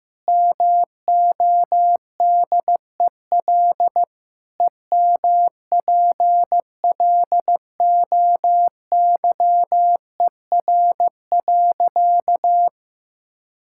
Morse code, heard as a keyed tone, about 16 wpm, 700 Hz. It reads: MODEL EMPLOYER.
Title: Sentences from Top 1600 Words 15wpm